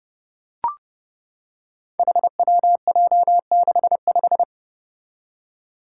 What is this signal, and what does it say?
Morse code, heard as a keyed tone, about 30 wpm, 700 Hz. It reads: HWJ65
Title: Five Challenging Letter-Number Combinations 30wpm